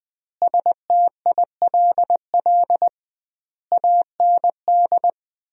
Morse code, heard as a keyed tone, about 20 wpm, 700 Hz. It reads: STILL AND
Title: 2x Word-Spacing - Sentences from Top 300 Words - 20wpm